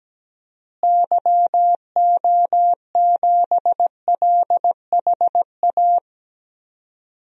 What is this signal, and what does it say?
Morse code, heard as a keyed tone, about 17 wpm, 700 Hz. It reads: YO7LHA
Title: DXCC-100 Call Signs - 6 Characters - Repeated 3x - 17wpm